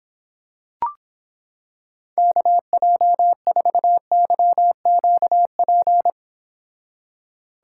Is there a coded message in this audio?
Yes: Morse code from 2.2 to 6.1 s, heard as a keyed tone, about 26 wpm, 700 Hz. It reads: KJ4YQP